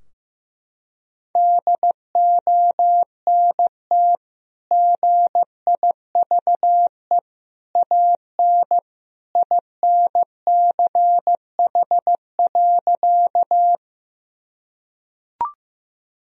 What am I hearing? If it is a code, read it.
Morse code, heard as a keyed tone, about 16 wpm, 700 Hz. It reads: DONT GIVE AN INCH.
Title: Sentences from Top 1000 Words - Review Part 3 - 15wpm